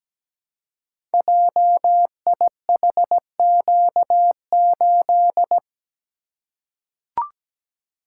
Morse code, heard as a keyed tone, about 17 wpm, 700 Hz. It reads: JIHQ8
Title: Code Groups - 5 Letter:Numbers - 17wpm